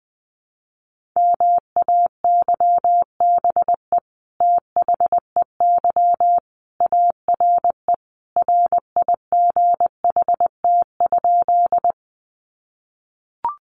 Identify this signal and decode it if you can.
Morse code, heard as a keyed tone, about 20 wpm, 700 Hz. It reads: MAYBE THEY ARE RIGHT?
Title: Sentences from Top 1000 Words 20wpm